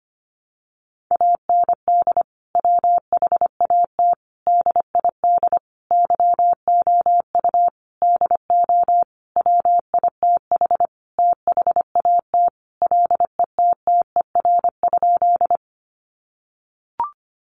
Morse code, heard as a keyed tone, about 25 wpm, 700 Hz. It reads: AND WHAT DID YOU DO WITH THAT LETTER?
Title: Sentences from Top 300 Words 25wpm